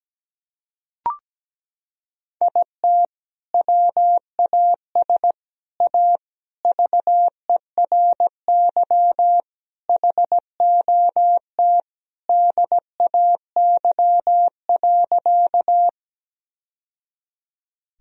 Morse code, heard as a keyed tone, about 17 wpm, 700 Hz. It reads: IT WAS A VERY HOT DAY.
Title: Sentences from Top 1000 Words - Review Part 3 - 17wpm